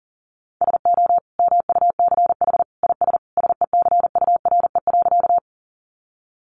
Morse code, heard as a keyed tone, about 40 wpm, 700 Hz. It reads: SO MUCH IS SECURE.
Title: Sentences from Top 1200 Words 40wpm